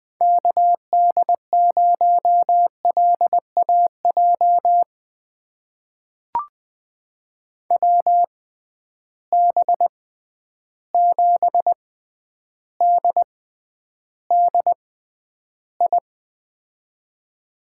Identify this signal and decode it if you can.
Morse code, heard as a keyed tone, about 20 wpm, spaced as for 10 wpm, 700 Hz. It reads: KD0LAJ WB7DDI
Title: USA Call Signs - 6 Characters - 6x Character Spacing - 20wpm